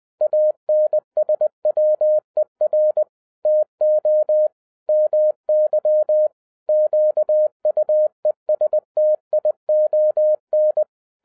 Morse code, heard as a keyed tone, about 20 wpm, 610 Hz. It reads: ANSWER TO MY QUESTION
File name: 2012-01-mp3 / n15-INVESTIGATION INTO THE POSSIBILITY OF WINNING PREFIX VERSION 3 AWARD-20wpm